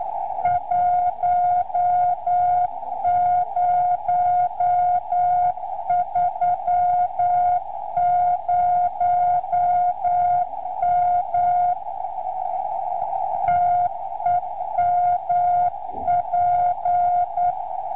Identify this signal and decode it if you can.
Morse code, heard as a keyed tone, about 9 wpm, 710 Hz, with radio noise. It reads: 1030M TEMP